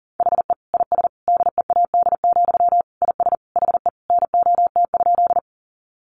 Morse code, heard as a keyed tone, about 40 wpm, 700 Hz. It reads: HE IS DEAD, IS HE NOT?
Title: Sentences from Top 1500 Words 40wpm